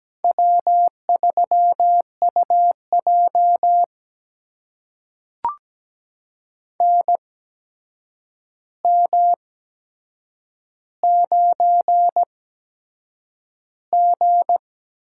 Morse code, heard as a keyed tone, about 17 wpm, spaced as for 7 wpm, 700 Hz. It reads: W3UJ NM9G
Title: USA Call Signs - 4 Characters - 8x Character Spacing - 17wpm